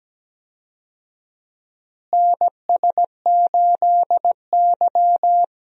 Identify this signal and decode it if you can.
Morse code, heard as a keyed tone, about 17 wpm, 700 Hz. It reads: NS8Y
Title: USA Call Signs - 4 Characters - Repeated 2x - 17wpm